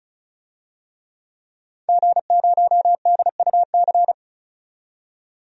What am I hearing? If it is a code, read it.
Morse code, heard as a keyed tone, about 36 wpm, 700 Hz. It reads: G0DUC
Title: DXCC-100 Call Signs - 5 Characters - 35wpm